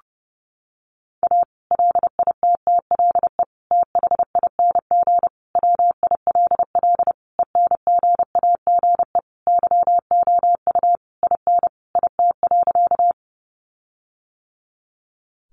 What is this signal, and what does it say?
Morse code, heard as a keyed tone, about 28 wpm, 700 Hz. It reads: A LITTLE THING WILL ENGAGE YOU IN IT.
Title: Sentences from Top 1500 Words 30wpm